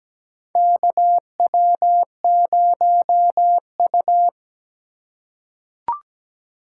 Morse code, heard as a keyed tone, about 17 wpm, 700 Hz. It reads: KW0U
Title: USA Call Signs - 3x Character Spacing - 17wpm